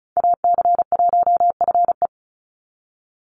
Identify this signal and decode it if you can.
Morse code, heard as a keyed tone, about 34 wpm, 700 Hz. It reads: AC1FE
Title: Call Signs Repeated 3x - 35wpm